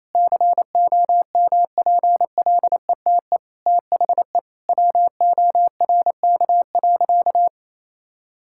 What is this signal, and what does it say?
Morse code, heard as a keyed tone, about 28 wpm, 700 Hz. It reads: COMPLETE THE WORK.
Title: Sentences from Top 1000 Words - Review Part 6 - 28wpm